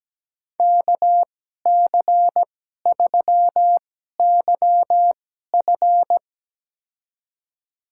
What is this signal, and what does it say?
Morse code, heard as a keyed tone, about 17 wpm, spaced as for 12 wpm, 700 Hz. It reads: KC3YF
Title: USA Call Signs - 5 Characters - 2x Character Spacing - 17wpm